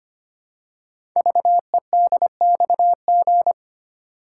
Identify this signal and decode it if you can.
Morse code, heard as a keyed tone, about 25 wpm, 700 Hz. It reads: VEDXG
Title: Code Groups - 5 Letters - 25wpm